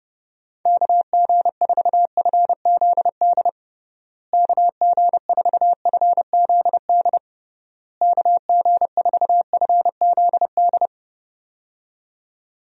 Morse code, heard as a keyed tone, about 32 wpm, 700 Hz. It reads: KG4FZD KG4FZD KG4FZD
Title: USA Call Signs - 6 Characters - Repeated 3x - 30wpm